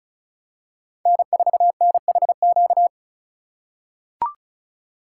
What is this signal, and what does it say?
Morse code, heard as a keyed tone, about 34 wpm, 700 Hz. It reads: N4NHQ
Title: USA Call Signs - 5 Characters - 4x Character Spacing - 35wpm